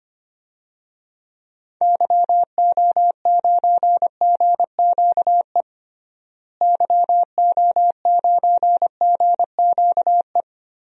Morse code, heard as a keyed tone, about 25 wpm, 700 Hz. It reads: YO9GQE YO9GQE
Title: DXCC-100 Call Signs - 6 Characters - Repeated 2x - 25wpm